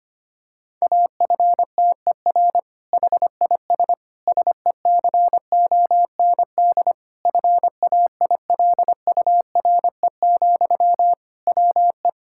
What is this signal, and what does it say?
Morse code, heard as a keyed tone, about 25 wpm, 700 Hz. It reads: AFTER HIS SECOND FAILURE, WE
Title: Sentences from Top 1000 Words - Review Part 6 - 25wpm